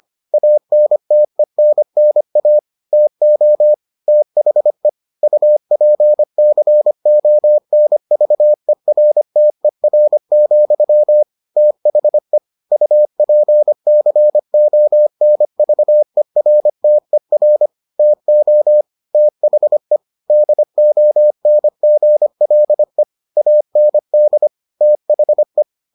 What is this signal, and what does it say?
Morse code, heard as a keyed tone, about 25 wpm, 600 Hz. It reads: ANTENNA TO THE UPCONVERTER, THE UPCONVERTER TO THE DONGLE AND THE